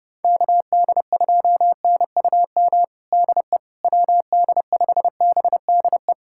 Morse code, heard as a keyed tone, about 30 wpm, 700 Hz. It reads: KD2NUM DE WD5BDE
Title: USA Callsign DE Callsign 30wpm